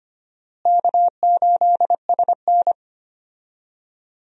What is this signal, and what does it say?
Morse code, heard as a keyed tone, about 25 wpm, 700 Hz. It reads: K8SN
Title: USA Call Signs - 4 Characters - Repeated 2x - 25wpm